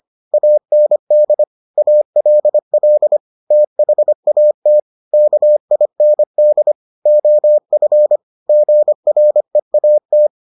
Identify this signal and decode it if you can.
Morse code, heard as a keyed tone, about 25 wpm, 600 Hz. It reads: AND ALL THAT KIND OF GREAT